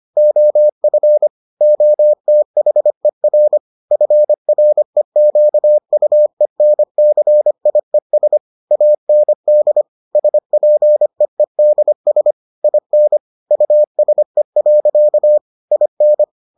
Morse code, heard as a keyed tone, about 25 wpm, 600 Hz. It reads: OF OTHER FREQUENCIES AND SPEEDS IN USE. IN